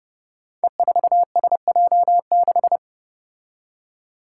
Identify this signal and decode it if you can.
Morse code, heard as a keyed tone, about 30 wpm, 700 Hz. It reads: E4SJ6
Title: Five Challenging Letter-Number Combinations 30wpm